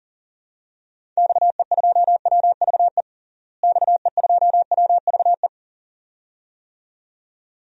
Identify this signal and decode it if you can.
Morse code, heard as a keyed tone, about 40 wpm, 700 Hz. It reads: XE2WVE XE2WVE
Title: DXCC-100 Call Signs - 6 Characters - Repeated 2x - 40wpm